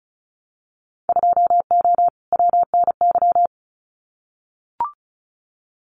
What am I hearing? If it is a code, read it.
Morse code, heard as a keyed tone, about 36 wpm, 700 Hz. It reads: 2O WNY
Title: Winter Field Day Exchanges Encore 35wpm